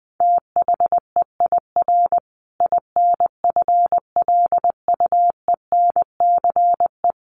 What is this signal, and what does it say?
Morse code, heard as a keyed tone, about 20 wpm, 700 Hz. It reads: THEIR INFLUENCE
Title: Sentences from Top 1400 Words 20wpm